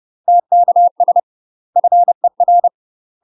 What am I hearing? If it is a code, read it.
Morse code, heard as a keyed tone, about 30 wpm, 700 Hz. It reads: TKS FER
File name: UK MASTER QSOs TRAINING / 30WPM / UK MASTER QSOs 0005